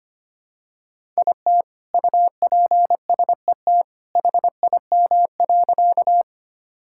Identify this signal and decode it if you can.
Morse code, heard as a keyed tone, about 25 wpm, 700 Hz. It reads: IT UPSET HIM.
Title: Sentences from Top 1000 Words - Review Part 5 - 25wpm